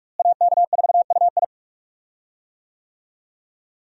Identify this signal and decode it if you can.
Morse code, heard as a keyed tone, about 42 wpm, 700 Hz. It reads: AK4UI
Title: USA Call Signs - 5 Characters - Repeated 3x - 45wpm